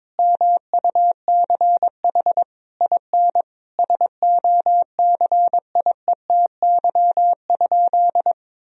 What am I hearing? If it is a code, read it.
Morse code, heard as a keyed tone, about 22 wpm, 700 Hz. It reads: MUCH IN SOCIETY?
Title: Sentences from Top 1000 Words - Review Part 4 - 22wpm